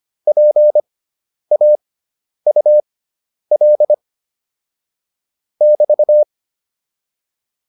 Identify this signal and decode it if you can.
Morse code, heard as a keyed tone, about 25 wpm, spaced as for 10 wpm, 600 Hz. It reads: PAUL <BT>
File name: QSO407 10